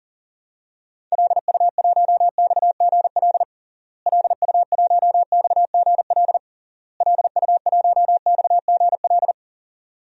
Morse code, heard as a keyed tone, about 40 wpm, 700 Hz. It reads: LU1XGL LU1XGL LU1XGL
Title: DXCC-100 Call Signs - 6 Characters - Repeated 3x - 40wpm